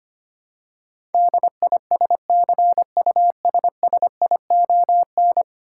Morse code, heard as a keyed tone, about 25 wpm, 700 Hz. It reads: DISCUSSION
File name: Ten Letter Words - 25wpm